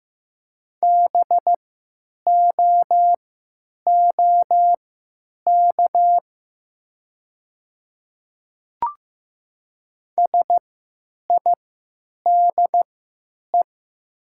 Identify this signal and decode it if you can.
Morse code, heard as a keyed tone, about 16 wpm, spaced as for 9 wpm, 700 Hz. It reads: BOOK SIDE
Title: 3x Character Spacing - Top 300 Words - 15wpm